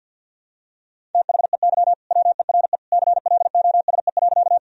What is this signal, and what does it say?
Morse code, heard as a keyed tone, about 51 wpm, 700 Hz. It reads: THEY WERE CLOSE.
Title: Sentences from Top 1000 Words - Review Part 1 - 50wpm